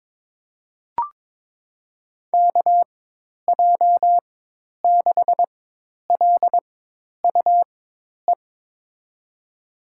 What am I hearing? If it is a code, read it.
Morse code, heard as a keyed tone, about 21 wpm, spaced as for 10 wpm, 700 Hz. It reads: KJ6LUE
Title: USA Call Signs - 4x Character Spacing - 22wpm